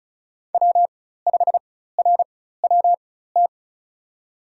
Morse code, heard as a keyed tone, about 36 wpm, spaced as for 16 wpm, 700 Hz. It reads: W5RWT
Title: USA Call Signs - 4x Character Spacing - 35wpm